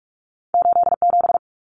Morse code, heard as a keyed tone, about 46 wpm, 700 Hz. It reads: 87
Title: Numbers - 2-Digit 45wpm